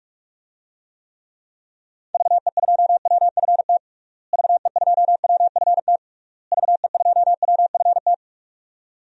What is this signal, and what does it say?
Morse code, heard as a keyed tone, about 46 wpm, 700 Hz. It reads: VE2WFT VE2WFT VE2WFT
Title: DXCC-100 Call Signs - 6 Characters - Repeated 3x - 45wpm